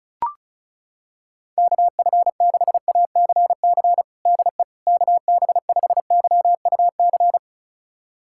Morse code, heard as a keyed tone, about 36 wpm, 700 Hz. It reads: KF6ACC DE KB5YUC
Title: USA Callsign DE Callsign 35wpm